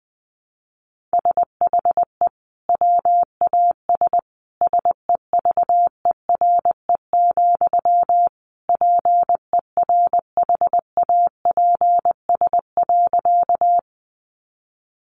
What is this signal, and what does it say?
Morse code, heard as a keyed tone, about 20 wpm, 700 Hz. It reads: SHE WAS SEVERE, PERHAPS.
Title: Sentences from Top 1400 Words 20wpm